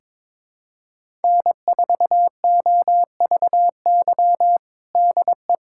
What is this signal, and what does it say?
Morse code, heard as a keyed tone, about 21 wpm, 700 Hz. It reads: N4OVY DE
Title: USA Callsign DE Callsign 22wpm